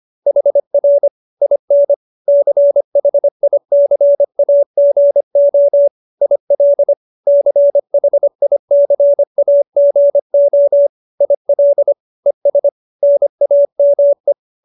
Morse code, heard as a keyed tone, about 25 wpm, 570 Hz. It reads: HR IN CHICAGO IL CHICAGO IL ES NAME